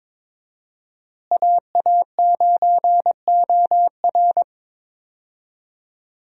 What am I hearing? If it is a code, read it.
Morse code, heard as a keyed tone, about 21 wpm, 700 Hz. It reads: AA9OR
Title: USA Call Signs - 5 Characters - Repeated 3x - 22wpm